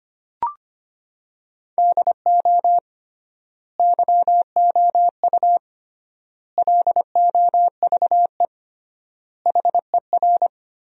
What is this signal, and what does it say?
Morse code, heard as a keyed tone, about 25 wpm, 700 Hz. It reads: DO YOU LOVE HER
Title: 3x Word-Spacing - Sentences from Top 400 Words - 25wpm